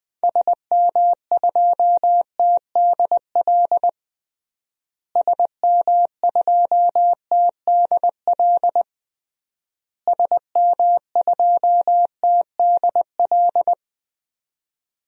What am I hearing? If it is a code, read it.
Morse code, heard as a keyed tone, about 20 wpm, 700 Hz. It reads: SM2TDL SM2TDL SM2TDL
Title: DXCC-100 Call Signs - 6 Characters - Repeated 3x - 20wpm